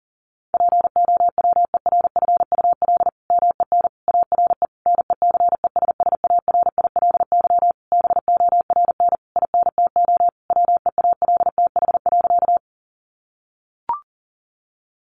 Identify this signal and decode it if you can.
Morse code, heard as a keyed tone, about 37 wpm, 700 Hz. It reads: POWERFUL MEN ARE NECESSARILY BORN INTO WEALTH.
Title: Sentences from Top 1400 Words 40wpm